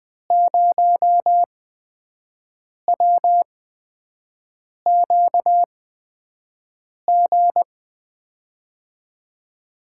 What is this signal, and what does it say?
Morse code, heard as a keyed tone, about 20 wpm, spaced as for 5 wpm, 700 Hz. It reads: 0WQG
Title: USA Call Signs - 6 Characters - 8x Character Spacing - 20wpm